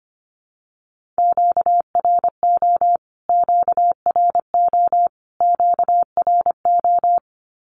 Morse code, heard as a keyed tone, about 25 wpm, 700 Hz. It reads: QRO QRO QRO
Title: Q-Codes Repeated 3x 25wpm